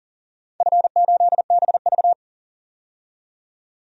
Morse code, heard as a keyed tone, about 40 wpm, 700 Hz. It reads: F8BV
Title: DXCC-100 Call Signs - 4 Characters - 40wpm